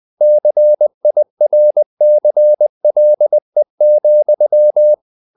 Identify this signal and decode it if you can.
Morse code, heard as a keyed tone, about 20 wpm, 600 Hz. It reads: CIRCLE,
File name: the old man and the sea part 2 20wpm